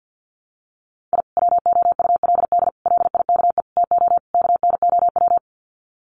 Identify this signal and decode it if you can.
Morse code, heard as a keyed tone, about 50 wpm, 700 Hz. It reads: I WOULD LIKE TO KNOW
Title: Sets of 5 Words Encore 50wpm